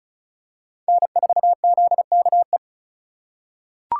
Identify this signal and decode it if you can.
Morse code, heard as a keyed tone, about 34 wpm, 700 Hz. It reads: N4ZKE
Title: USA Call Signs - 3x Character Spacing - 35wpm